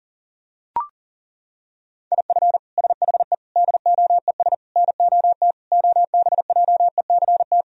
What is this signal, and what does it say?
Morse code, heard as a keyed tone, about 37 wpm, 700 Hz. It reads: IF SHE DOES NOT OBJECT
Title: Sentences from Top 1000 Words - Review Part 3 - 40wpm